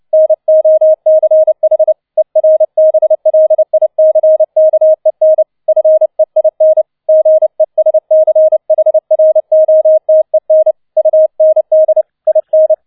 Morse code, heard as a keyed tone, about 29 wpm, 630 Hz. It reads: NOCH ERBLICKEN FEIN GESCHROTEN UND IN
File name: 007 Letzter Streich 30wpm